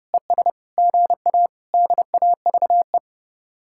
Morse code, heard as a keyed tone, about 40 wpm, 700 Hz. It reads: ES GA DAVE